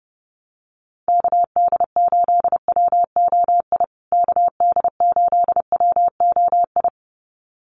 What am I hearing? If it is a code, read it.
Morse code, heard as a keyed tone, about 30 wpm, 700 Hz. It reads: KD8WOI KD8WOI
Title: USA Call Signs Repeated 2x 30wpm